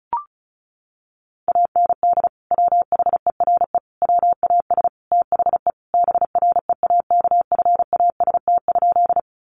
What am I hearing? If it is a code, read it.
Morse code, heard as a keyed tone, about 34 wpm, 700 Hz. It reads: AND WHERE WAS THE BREAKFAST?
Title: Sentences from Top 1500 Words 35wpm